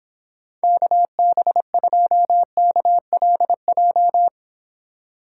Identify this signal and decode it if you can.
Morse code, heard as a keyed tone, about 29 wpm, 700 Hz. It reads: KB2KLJ